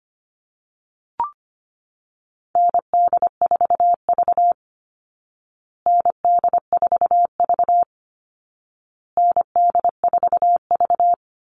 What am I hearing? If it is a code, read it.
Morse code, heard as a keyed tone, about 25 wpm, 700 Hz. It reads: ND4V ND4V ND4V
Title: Call Signs Repeated 2x - 25wpm